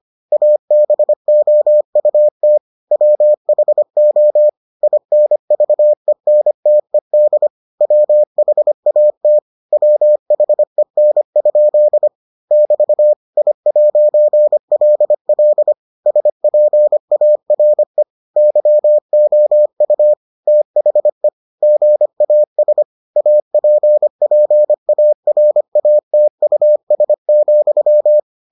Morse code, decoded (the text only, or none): ABOUT WHO INVENTED WHAT WHEN? <BT> I'LL SPARE YOU THE GAS APPARATUS,